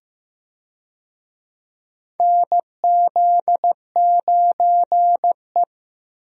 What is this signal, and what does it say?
Morse code, heard as a keyed tone, about 15 wpm, 700 Hz. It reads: NZ9E
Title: USA Call Signs - 4 Characters - Repeated 3x - 15wpm